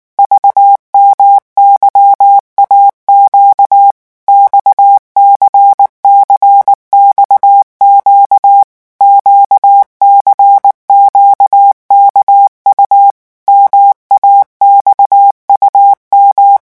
Morse code, heard as a keyed tone, about 19 wpm, 800 Hz. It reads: VMYAQ XCCXQ QCQKU MAXUM
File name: morse mp3 / revision2-vitesse 18 mots